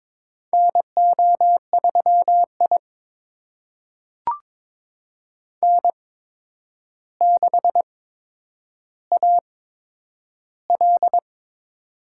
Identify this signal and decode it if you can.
Morse code, heard as a keyed tone, about 22 wpm, spaced as for 9 wpm, 700 Hz. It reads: NO3I N6AL
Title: USA Call Signs - 4 Characters - 8x Character Spacing - 22wpm